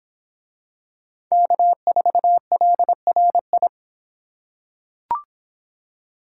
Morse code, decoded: K4LRI

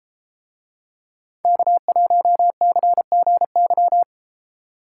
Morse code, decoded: K1CGY

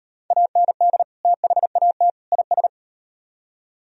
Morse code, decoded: AND THAT IS

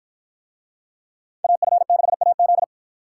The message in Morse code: AF6AZ

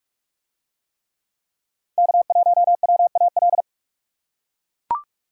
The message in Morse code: K1WAL

